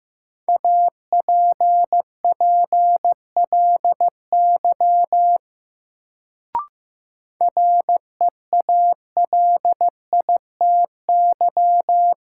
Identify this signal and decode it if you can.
Morse code, heard as a keyed tone, about 15 wpm, 700 Hz. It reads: APPLY REALITY
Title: Sentences from Top 1200 Words - New Word Drill Alt - 15wpm